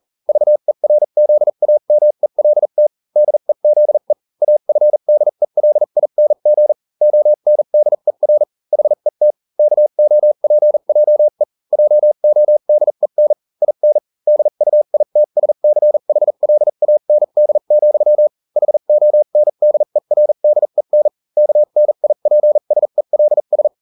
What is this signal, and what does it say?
Morse code, heard as a keyed tone, about 40 wpm, spaced as for 35 wpm, 600 Hz. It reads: VERZAMELT DEZE AFDELING ONDER HET KOPJE JODEN IN DUITSCHLAND, HONDERDEN KNIPSELS